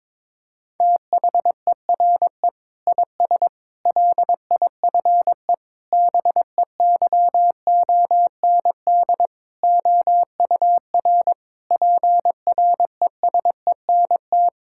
Morse code, heard as a keyed tone, about 23 wpm, 700 Hz. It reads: THERE IS LIFE BEYOND OUR PRESENT
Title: Sentences from Top 1000 Words - Review Part 6 - 22wpm